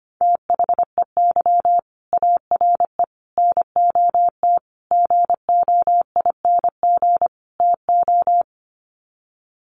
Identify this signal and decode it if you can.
Morse code, heard as a keyed tone, about 25 wpm, 700 Hz. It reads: THEY ARE NOT GOING TO
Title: Sets of 5 Words Encore2 25wpm